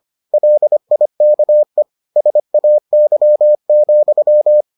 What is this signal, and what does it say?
Morse code, heard as a keyed tone, about 25 wpm, 600 Hz. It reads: LIKE SAY,